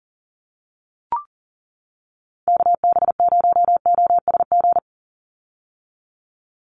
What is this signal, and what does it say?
Morse code, heard as a keyed tone, about 40 wpm, 690 Hz. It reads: KB0OSG
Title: USA Call Signs 40wpm